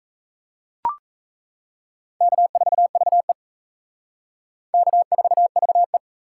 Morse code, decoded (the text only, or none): K4VE K4VE